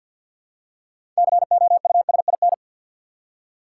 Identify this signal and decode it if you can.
Morse code, heard as a keyed tone, about 52 wpm, 700 Hz. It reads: COUSIN
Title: Six Letter Words - 50wpm